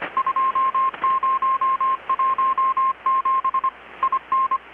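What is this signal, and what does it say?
Morse code, heard as a keyed tone, about 25 wpm, 1060 Hz. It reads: 2017 IN
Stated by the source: Morse from KPH's Night of Nights XX maritime radiotelegraphy event, received in Chiapas, Mexico about 3,800 km away on a Kenwood R-600 with a grounded 10 m long-wire antenna.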